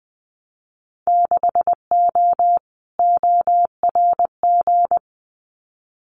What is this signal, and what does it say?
Morse code, heard as a keyed tone, about 20 wpm, 700 Hz. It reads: 6O ORG
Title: Winter Field Day Exchanges Encore 20wpm